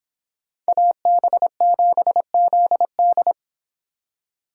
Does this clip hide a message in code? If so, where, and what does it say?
Yes: Morse code from 0.7 to 3.3 s, heard as a keyed tone, about 26 wpm, 700 Hz. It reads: AB7ZD